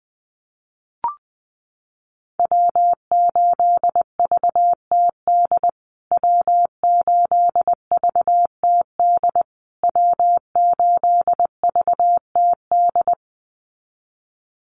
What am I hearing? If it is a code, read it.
Morse code, heard as a keyed tone, about 20 wpm, 700 Hz. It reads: W8VTD W8VTD W8VTD
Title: USA Call Signs Repeated 3x 20wpm